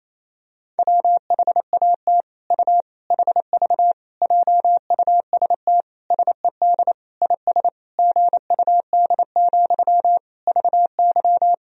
Morse code, heard as a keyed tone, about 28 wpm, 700 Hz. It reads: WHAT U HV JUST SED IS GUD, VY
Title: Sentences from Top 200 Words and Abbreviations 28wpm